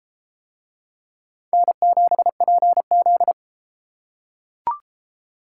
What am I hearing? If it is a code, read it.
Morse code, heard as a keyed tone, about 29 wpm, 700 Hz. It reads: N7PZ